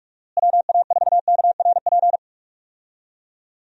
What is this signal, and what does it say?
Morse code, heard as a keyed tone, about 47 wpm, 700 Hz. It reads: WA4KRP
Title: USA Call Signs - 6 Characters - Repeated 3x - 45wpm